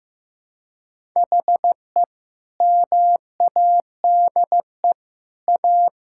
Morse code, heard as a keyed tone, about 15 wpm, 700 Hz. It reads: HE MADE A
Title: Sentences from Top 1000 Words - Review Part 6 - 15wpm